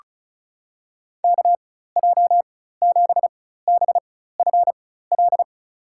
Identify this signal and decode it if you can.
Morse code, heard as a keyed tone, about 34 wpm, spaced as for 16 wpm, 700 Hz. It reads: KJ7BFL
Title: USA Call Signs - 6 Characters - 4x Character Spacing - 35wpm